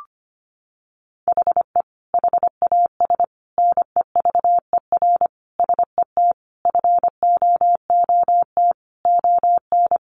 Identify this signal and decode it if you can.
Morse code, heard as a keyed tone, about 25 wpm, 700 Hz. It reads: HE HAS NEVER SET FOOT ON
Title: Sentences from Top 1500 Words 25wpm